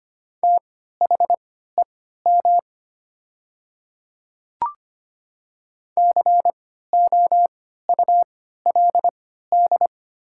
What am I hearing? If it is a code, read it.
Morse code, heard as a keyed tone, about 25 wpm, spaced as for 14 wpm, 700 Hz. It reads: THEM COULD
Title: 3x Character Spacing - Top 200 Words - 25wpm